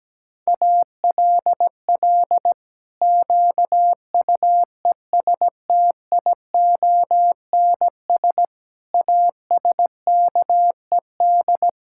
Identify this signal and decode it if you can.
Morse code, heard as a keyed tone, about 17 wpm, 700 Hz. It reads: ALL QUESTIONS ASKED
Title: Pangram Challenge v2 - 17wpm